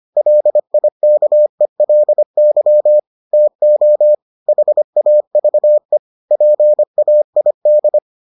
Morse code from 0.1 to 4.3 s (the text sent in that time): LIKELY TO